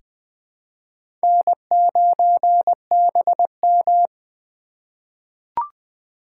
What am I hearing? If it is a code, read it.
Morse code, heard as a keyed tone, about 20 wpm, 700 Hz. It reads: N9BM
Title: USA Call Signs - 4 Characters - 3x Character Spacing - 20wpm